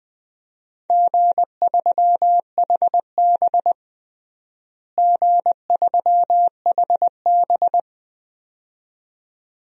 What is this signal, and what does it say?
Morse code, heard as a keyed tone, about 20 wpm, 700 Hz. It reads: G3HB G3HB
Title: DXCC-100 Call Signs - 4 Characters - Repeated 2x - 20wpm